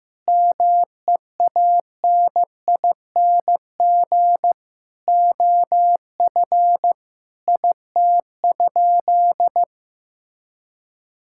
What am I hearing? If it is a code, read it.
Morse code, heard as a keyed tone, about 15 wpm, 700 Hz. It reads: MEANING OF IT?
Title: Sentences from Top 1000 Words - Review Part 6 - 15wpm